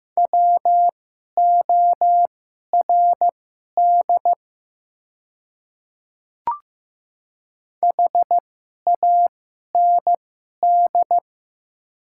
Morse code, heard as a keyed tone, about 15 wpm, spaced as for 11 wpm, 700 Hz. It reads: WORD HAND
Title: 2x Character Spacing - Top 200 Words - 15wpm